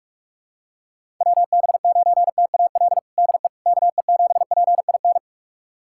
Morse code, heard as a keyed tone, about 47 wpm, 700 Hz. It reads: WB9TAL DE KE7PIN
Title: USA Callsign DE Callsign - 6 Characters - 45wpm